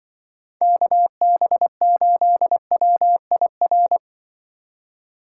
Morse code, heard as a keyed tone, about 24 wpm, 700 Hz. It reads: KB8WIR
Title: Speed Racing - USA 2x3 Call Signs 20wpm